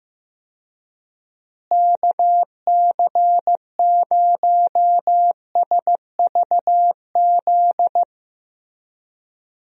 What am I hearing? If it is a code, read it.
Morse code, heard as a keyed tone, about 15 wpm, 700 Hz. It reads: KC0SVZ
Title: USA Call Signs - 6 Characters - Repeated 3x - 15wpm